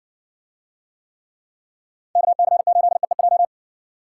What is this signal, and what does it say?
Morse code, heard as a keyed tone, about 60 wpm, 700 Hz. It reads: KC8EEJ